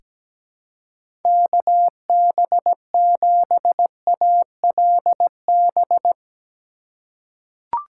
Morse code, heard as a keyed tone, about 17 wpm, 700 Hz. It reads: KB7ALB
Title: USA Call Signs - 6 Characters - 4x Character Spacing - 17wpm